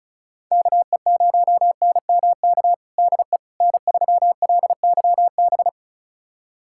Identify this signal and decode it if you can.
Morse code, heard as a keyed tone, about 34 wpm, 700 Hz. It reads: KE0NMK DE N3LYB
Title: USA Callsign DE Callsign 35wpm